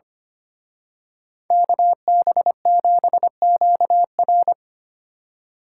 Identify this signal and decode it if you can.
Morse code, heard as a keyed tone, about 29 wpm, 700 Hz. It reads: KB7QR
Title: Speed Racing - USA 2x2 Call Signs 25wpm